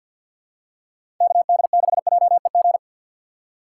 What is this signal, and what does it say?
Morse code, heard as a keyed tone, about 53 wpm, 700 Hz. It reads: KD6JEG
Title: USA Call Signs - 2x Character Spacing - 50wpm